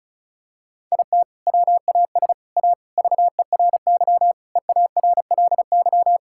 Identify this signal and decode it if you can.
Morse code, heard as a keyed tone, about 34 wpm, 700 Hz. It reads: IT WAS A VERY EARLY
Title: Sentences from Top 1000 Words - Review Part 6 - 35wpm